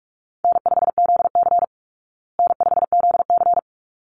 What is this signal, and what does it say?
Morse code, heard as a keyed tone, about 44 wpm, 700 Hz. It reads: N5ZC N5ZC
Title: Call Signs Repeated 3x - 45wpm